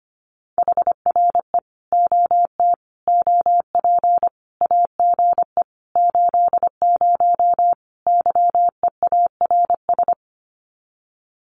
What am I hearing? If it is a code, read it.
Morse code, heard as a keyed tone, about 25 wpm, 700 Hz. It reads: HRE OT OP AGE 80 YEARS
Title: Common QSO Phrases 25wpm